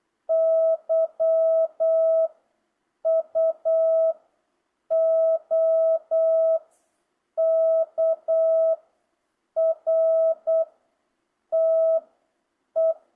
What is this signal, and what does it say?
Morse code, heard as a keyed tone, about 8 wpm, 650 Hz. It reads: Y U O K R T E